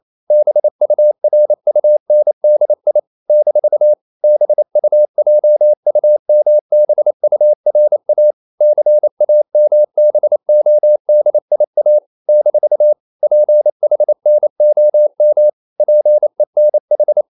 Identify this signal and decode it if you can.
Morse code, heard as a keyed tone, about 28 wpm, 600 Hz. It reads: BURUNDI - BUJUMBURA CAMBODIA - PHNOM PENH
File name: CW / Practice / NationalCapitals28